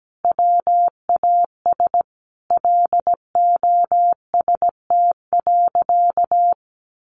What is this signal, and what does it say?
Morse code, heard as a keyed tone, about 17 wpm, 700 Hz. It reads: WAS LOST.
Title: Sentences from Top 1400 Words 17wpm